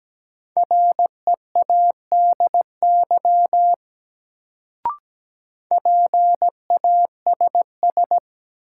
Morse code, heard as a keyed tone, about 17 wpm, 700 Hz. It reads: READY PASS
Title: Sentences from Top 400 Words - New Word Drill - 17wpm